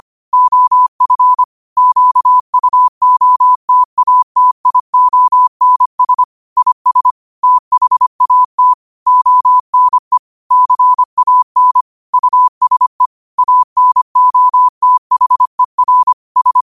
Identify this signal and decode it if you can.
Morse code, heard as a keyed tone, about 25 wpm, 1000 Hz. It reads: OF QUOTATIONS IS THAT ONE CAN USE ANOTHER S
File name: qotdfiles / 25wpm